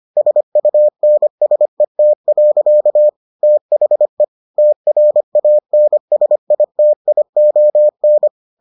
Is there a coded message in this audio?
Yes: Morse code from 0.2 to 8.3 s, heard as a keyed tone, about 25 wpm, 600 Hz. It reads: SUNSET. THE TRANSITION